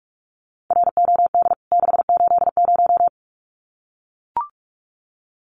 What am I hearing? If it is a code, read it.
Morse code, heard as a keyed tone, about 44 wpm, 700 Hz. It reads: ROD 680
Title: CWT Exchange 45wpm